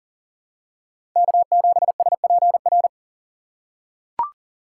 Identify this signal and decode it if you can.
Morse code, heard as a keyed tone, about 40 wpm, 700 Hz. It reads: K7SPR
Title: USA Call Signs - 5 Characters - 2x Character Spacing - 40wpm